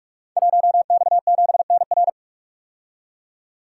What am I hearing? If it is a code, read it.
Morse code, heard as a keyed tone, about 44 wpm, 700 Hz. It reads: 1X7NR
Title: Code Groups - 5 Letter:Numbers - 45wpm